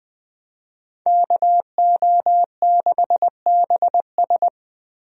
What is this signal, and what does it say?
Morse code, heard as a keyed tone, about 20 wpm, 700 Hz. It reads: KO6BS
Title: USA Call Signs - 5 Characters - Repeated 3x - 20wpm